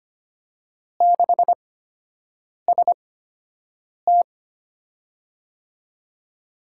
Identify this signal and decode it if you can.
Morse code, heard as a keyed tone, about 25 wpm, spaced as for 7 wpm, 700 Hz. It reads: 6ST
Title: USA Call Signs - 4 Characters - 8x Character Spacing - 25wpm